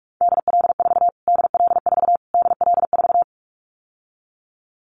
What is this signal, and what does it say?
Morse code, heard as a keyed tone, about 44 wpm, 700 Hz. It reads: DL4 DL4 DL4
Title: Callsign Prefixes for Top 25 DXCC - Repeated 3x - 45wpm